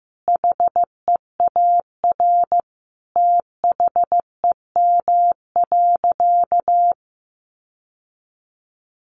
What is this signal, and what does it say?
Morse code, heard as a keyed tone, about 14 wpm, 700 Hz. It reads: HEAR THEM.
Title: Sentences from Top 600 Words 15wpm